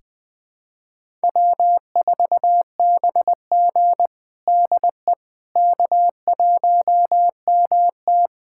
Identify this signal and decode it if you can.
Morse code, heard as a keyed tone, about 20 wpm, 700 Hz. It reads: W4BG DE K1MT
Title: USA Callsign DE Callsign - 4 Characters - 20wpm